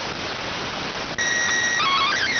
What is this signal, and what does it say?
SSTV (slow-scan television), PD 290.